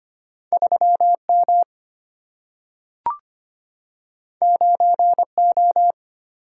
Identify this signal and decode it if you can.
Morse code, heard as a keyed tone, about 25 wpm, 700 Hz. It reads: 3M 9O
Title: Single-Letter Single-Number 25wpm